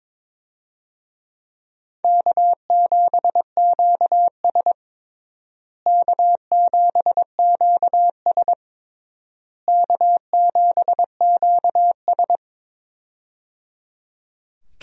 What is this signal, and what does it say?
Morse code, heard as a keyed tone, about 22 wpm, 700 Hz. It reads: K7QS K7QS K7QS
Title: USA Call Signs - 4 Characters - Repeated 3x - 22wpm